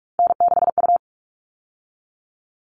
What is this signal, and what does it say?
Morse code, heard as a keyed tone, about 44 wpm, 700 Hz. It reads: N6U
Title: All Cut Numbers - 3-Digit 45wpm